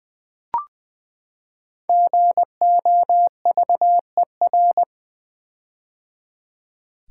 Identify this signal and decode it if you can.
Morse code, heard as a keyed tone, about 20 wpm, 700 Hz. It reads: GOVER
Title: Build A Word - 5 Letters At A Time - 10-Letter Words - 20wpm